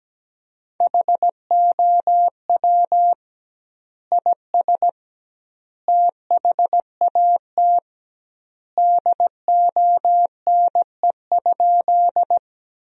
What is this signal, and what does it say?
Morse code, heard as a keyed tone, about 17 wpm, 700 Hz. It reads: HOW IS THAT DONE?
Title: 2x Word-Spacing - Sentences from Top 500 Words - 17wpm